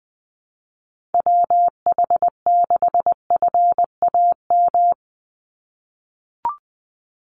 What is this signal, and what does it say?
Morse code, heard as a keyed tone, about 20 wpm, 700 Hz. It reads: WH6FAM
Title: USA Call Signs 20wpm